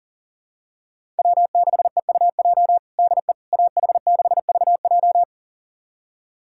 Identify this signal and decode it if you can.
Morse code, heard as a keyed tone, about 40 wpm, 700 Hz. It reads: W6EUJ DE AH6VJ
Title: USA Callsign DE Callsign - 5 Characters - 40wpm